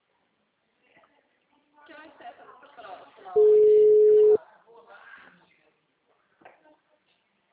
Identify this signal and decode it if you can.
Ringback tone (Europe, 425 Hz), with speech.